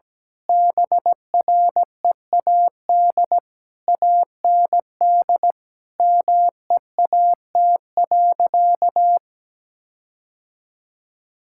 Morse code, heard as a keyed tone, about 17 wpm, 700 Hz. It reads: BREAD AND MEAT.
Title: Sentences from Top 1600 Words 17wpm